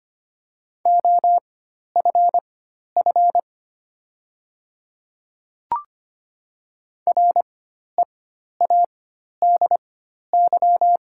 Morse code, heard as a keyed tone, about 25 wpm, spaced as for 12 wpm, 700 Hz. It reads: OFF READY
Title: 4x Character Spacing - Top 500 Words - 25wpm